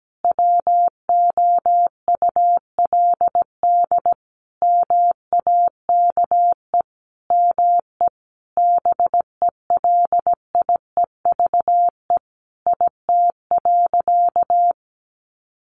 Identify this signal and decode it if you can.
Morse code, heard as a keyed tone, about 17 wpm, 700 Hz. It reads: WOULD MAKE ME BELIEVE IT.
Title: Sentences from Top 1300 Words 17wpm